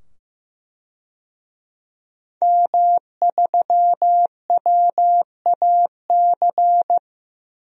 Morse code, heard as a keyed tone, about 15 wpm, 700 Hz. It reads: M3WAC
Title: DXCC-100 Call Signs - 5 Characters - 15wpm